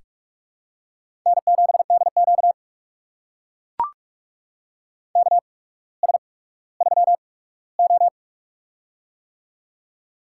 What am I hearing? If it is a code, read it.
Morse code, heard as a keyed tone, about 43 wpm, spaced as for 19 wpm, 700 Hz. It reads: N7DQ KS3O
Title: USA Call Signs - 4 Characters - 8x Character Spacing - 45wpm